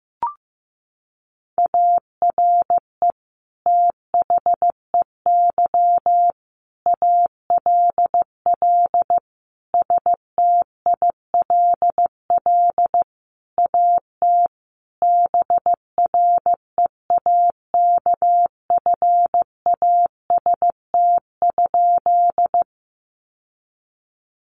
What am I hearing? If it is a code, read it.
Morse code, heard as a keyed tone, about 15 wpm, 700 Hz. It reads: ARE THEY ALL STILL AT BREAKFAST?
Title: Sentences from Top 1500 Words 15wpm